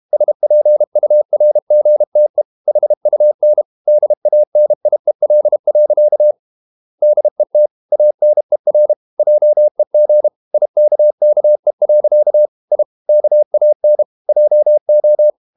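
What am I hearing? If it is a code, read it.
Morse code, heard as a keyed tone, about 32 wpm, 600 Hz. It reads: SPURGTE HUN DANIEL. DET ANER JEG IKKE. I KAN JO
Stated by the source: High-speed QRQ CW at 32 WPM.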